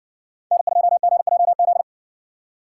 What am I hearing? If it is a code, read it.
Morse code, heard as a keyed tone, about 59 wpm, 700 Hz. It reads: N2GJZ